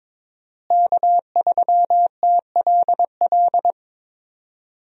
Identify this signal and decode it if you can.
Morse code, heard as a keyed tone, about 22 wpm, 700 Hz. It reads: K3TLL
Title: USA Call Signs - 5 Characters - Repeated 3x - 22wpm